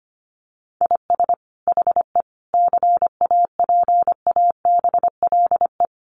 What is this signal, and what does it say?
Morse code, heard as a keyed tone, about 25 wpm, 700 Hz. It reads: IS HE CAPABLE
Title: Sentences from Top 1200 Words 25wpm